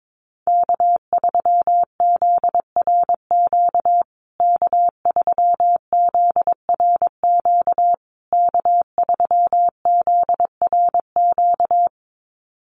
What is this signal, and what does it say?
Morse code, heard as a keyed tone, about 22 wpm, 690 Hz. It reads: K3ZRQ K3ZRQ K3ZRQ
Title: USA Call Signs Repeated 3x 22wpm